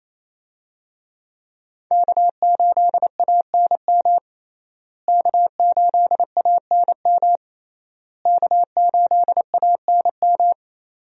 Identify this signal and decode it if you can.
Morse code, heard as a keyed tone, about 27 wpm, 700 Hz. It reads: K8ANM K8ANM K8ANM
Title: USA Call Signs - 5 Characters - Repeated 3x - 28wpm